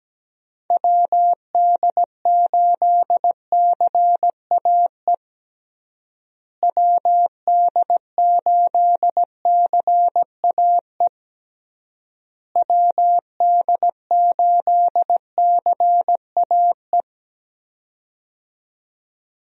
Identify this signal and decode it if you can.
Morse code, heard as a keyed tone, about 17 wpm, 700 Hz. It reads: WD8CAE WD8CAE WD8CAE
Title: USA Call Signs - 6 Characters - Repeated 3x - 17wpm